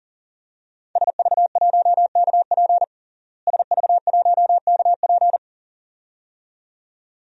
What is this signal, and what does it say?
Morse code, heard as a keyed tone, about 40 wpm, 690 Hz. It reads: SV1KP SV1KP
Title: DXCC-100 Call Signs - 5 Characters - Repeated 2x - 40wpm